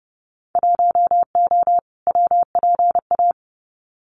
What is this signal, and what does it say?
Morse code, heard as a keyed tone, about 30 wpm, 700 Hz. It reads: 1O WPA
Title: Winter Field Day Exchanges 30wpm